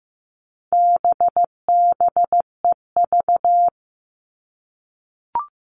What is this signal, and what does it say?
Morse code, heard as a keyed tone, about 15 wpm, 700 Hz. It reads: BBEV
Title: All Cut Numbers - 4-Digit 15wpm